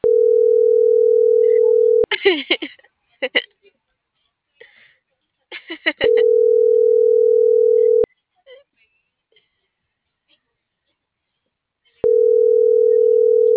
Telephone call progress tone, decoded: ringback tone

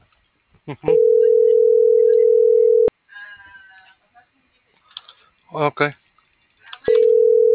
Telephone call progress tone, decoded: ringback tone

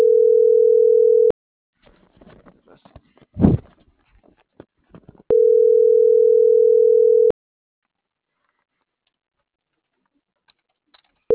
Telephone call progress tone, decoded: ringback tone

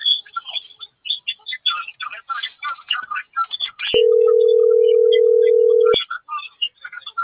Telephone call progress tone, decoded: ringback tone

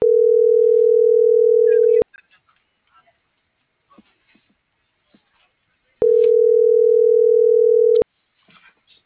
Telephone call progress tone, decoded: ringback tone

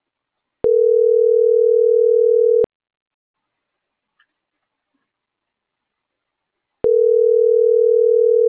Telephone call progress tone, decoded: ringback tone

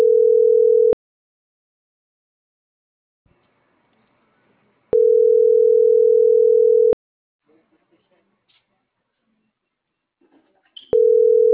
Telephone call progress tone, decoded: ringback tone

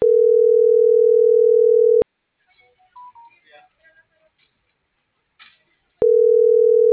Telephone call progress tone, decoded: ringback tone

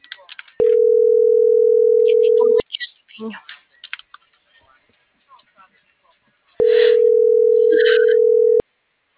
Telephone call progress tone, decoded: ringback tone

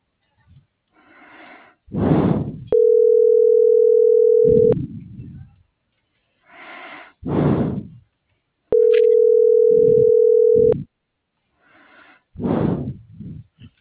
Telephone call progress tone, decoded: ringback tone

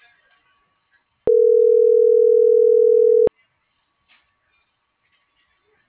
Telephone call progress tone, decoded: ringback tone

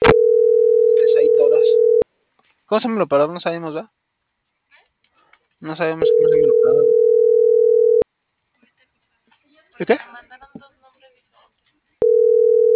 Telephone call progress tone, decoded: ringback tone

